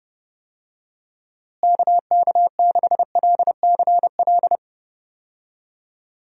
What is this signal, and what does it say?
Morse code, heard as a keyed tone, about 30 wpm, 700 Hz. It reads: KK6LCL
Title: USA Call Signs - 6 Characters - Repeated 2x - 30wpm